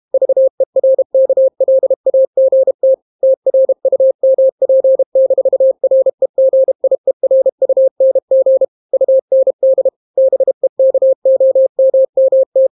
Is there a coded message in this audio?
Yes: Morse code from 0.1 to 12.7 s, heard as a keyed tone, about 32 wpm, 530 Hz. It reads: VERKLAGT TRUMP-REGIERUNG UND BEKOMMT